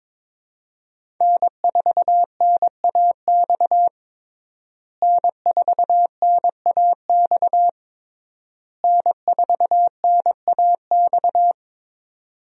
Morse code, heard as a keyed tone, about 22 wpm, 700 Hz. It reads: N4NAX N4NAX N4NAX
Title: USA Call Signs - 5 Characters - Repeated 3x - 22wpm